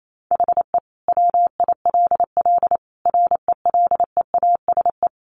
Morse code, heard as a keyed tone, about 28 wpm, 700 Hz. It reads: HE WILL RELEASE
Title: Sentences from Top 1100 Words 28wpm